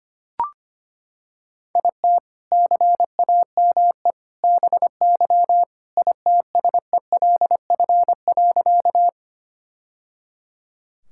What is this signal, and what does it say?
Morse code, heard as a keyed tone, about 25 wpm, 700 Hz. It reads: IT CAME BY ITSELF.
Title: Sentences from Top 1000 Words - Review Part 5 - 25wpm